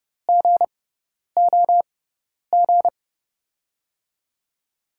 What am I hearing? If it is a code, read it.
Morse code, heard as a keyed tone, about 31 wpm, spaced as for 10 wpm, 700 Hz. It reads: GOG
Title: USA Call Signs - 5 Characters - 6x Character Spacing - 30wpm